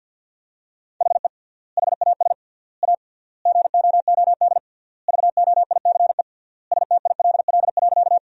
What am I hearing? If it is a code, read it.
Morse code, heard as a keyed tone, about 50 wpm, 700 Hz. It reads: HE HAS A GOOD VOICE STILL.
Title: 3x Word-Spacing - Sentences from Top 500 Words - 50wpm